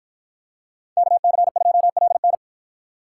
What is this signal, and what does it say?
Morse code, heard as a keyed tone, about 53 wpm, 700 Hz. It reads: KC2LN